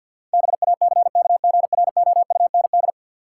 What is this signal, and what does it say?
Morse code, heard as a keyed tone, about 53 wpm, 700 Hz. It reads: BACKGROUND
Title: Sentences from Top 1300 Words - New Word Drill Alt - 50wpm